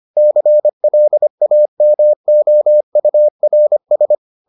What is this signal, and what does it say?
Morse code, heard as a keyed tone, about 25 wpm, 600 Hz. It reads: CLAMOURS